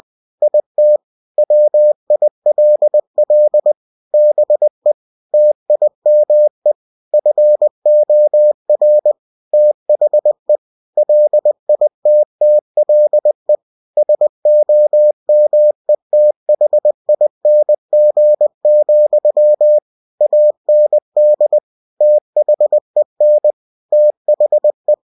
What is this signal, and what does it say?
Morse code, heard as a keyed tone, about 20 wpm, 600 Hz. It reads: IT WILL BE TIME FOR THE LITTLE SOMETHING, AND THEN THE
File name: cw-books / winnie-the-pooh / winnie-20wpm-page-0076